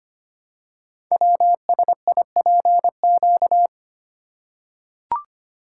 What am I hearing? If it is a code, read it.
Morse code, heard as a keyed tone, about 25 wpm, 700 Hz. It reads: WSIPQ
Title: Code Groups - 5 Letters - 25wpm